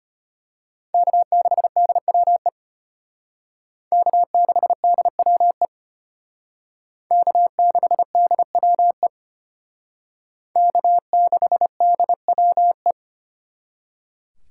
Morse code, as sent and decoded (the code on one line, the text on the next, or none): -.- -.... -.. .-- . / -.- -.... -.. .-- . / -.- -.... -.. .-- . / -.- -.... -.. .-- .
K6DWE K6DWE K6DWE K6DWE